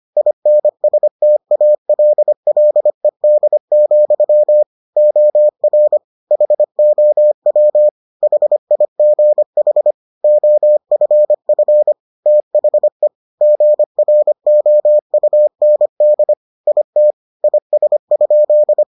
Morse code, heard as a keyed tone, about 25 wpm, 600 Hz. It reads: INSTALLED, OR HOW HIGH OFF THE GROUND IT IS?